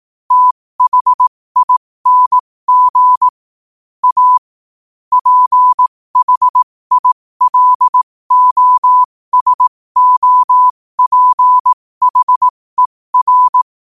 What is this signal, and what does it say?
Morse code, heard as a keyed tone, about 18 wpm, spaced as for 15 wpm, 1000 Hz. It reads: THING A PHILOSOPHER
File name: qotdfiles / 15wpm